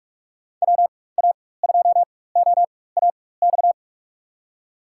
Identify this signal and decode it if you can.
Morse code, heard as a keyed tone, about 47 wpm, spaced as for 21 wpm, 700 Hz. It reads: WA2OAX
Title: USA Call Signs - 6 Characters - 4x Character Spacing - 45wpm